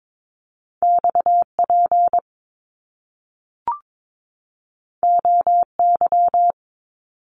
Morse code, heard as a keyed tone, about 22 wpm, 700 Hz. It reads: XP OY
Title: Top Two Letter Combinations in Words 22wpm